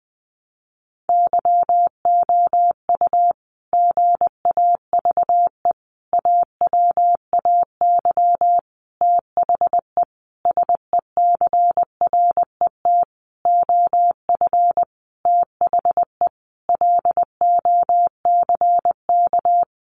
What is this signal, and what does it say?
Morse code, heard as a keyed tone, about 20 wpm, 700 Hz. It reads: YOU GAVE AWAY THE SECRET OF THE LOCK
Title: Sentences from Top 1300 Words 20wpm